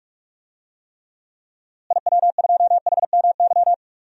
Morse code, heard as a keyed tone, about 46 wpm, 700 Hz. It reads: IW2HMY
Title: DXCC-100 Call Signs - 6 Characters - Repeated 2x - 45wpm